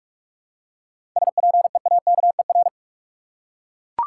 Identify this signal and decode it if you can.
Morse code, heard as a keyed tone, about 41 wpm, 700 Hz. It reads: SPEAKER
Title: Seven Letter Words - 45wpm